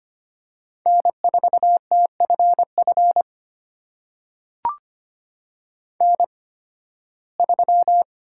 Morse code, heard as a keyed tone, about 25 wpm, spaced as for 16 wpm, 700 Hz. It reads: N4TFF N3
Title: USA Call Signs - 5 Characters - 8x Character Spacing - 25wpm